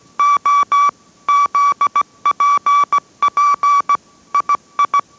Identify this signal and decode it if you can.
Morse code, heard as a keyed tone, about 17 wpm, 1150 Hz. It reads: O ZPP II